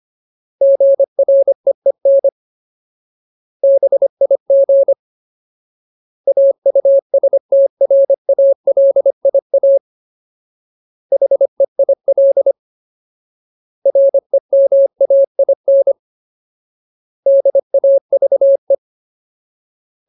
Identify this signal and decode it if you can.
Morse code, heard as a keyed tone, about 25 wpm, 550 Hz. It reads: GREEN BIG AUSTRALIA HEIL REMAIN DAVE
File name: Ultimate Word List / 25WPM Ultimate